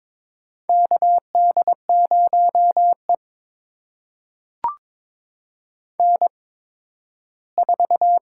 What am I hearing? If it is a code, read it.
Morse code, heard as a keyed tone, about 22 wpm, spaced as for 13 wpm, 700 Hz. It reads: KD0E N4
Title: USA Call Signs - 4 Characters - 8x Character Spacing - 22wpm